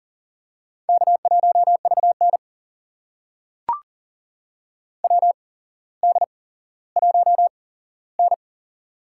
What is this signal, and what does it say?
Morse code, heard as a keyed tone, about 40 wpm, spaced as for 17 wpm, 700 Hz. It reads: K1VN WD1N
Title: USA Call Signs - 4 Characters - 8x Character Spacing - 40wpm